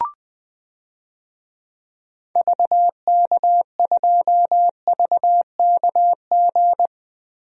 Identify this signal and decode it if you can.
Morse code, heard as a keyed tone, about 20 wpm, 700 Hz. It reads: VK2VKG
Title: DXCC-100 Call Signs - 6 Characters - Repeated 2x - 20wpm